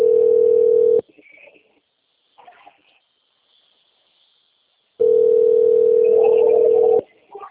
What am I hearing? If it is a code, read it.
Ringback tone (North America), 440 + 480 Hz.